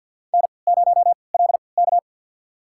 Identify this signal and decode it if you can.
Morse code, heard as a keyed tone, about 51 wpm, spaced as for 28 wpm, 700 Hz. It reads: N0LK
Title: USA Call Signs - 4 Characters - 3x Character Spacing - 50wpm